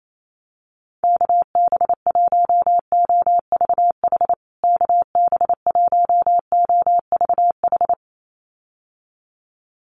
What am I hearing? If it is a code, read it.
Morse code, heard as a keyed tone, about 28 wpm, 700 Hz. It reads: KB1OVH KB1OVH
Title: USA Call Signs Repeated 2x 28wpm